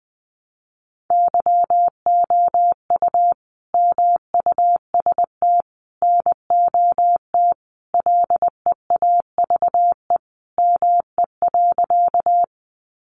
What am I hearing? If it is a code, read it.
Morse code, heard as a keyed tone, about 20 wpm, 700 Hz. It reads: YOU MUST NOT LEAVE ME.
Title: Sentences from Top 600 Words 20wpm